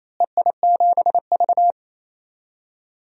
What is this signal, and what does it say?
Morse code, heard as a keyed tone, about 27 wpm, 700 Hz. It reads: EI7V
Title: DXCC-100 Call Signs - 4 Characters - Repeated 3x - 28wpm